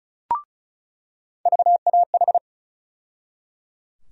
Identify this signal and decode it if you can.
Morse code, heard as a keyed tone, about 35 wpm, 700 Hz. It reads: VAH
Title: USA Call Sign Postfixes - 3 Letters 35wpm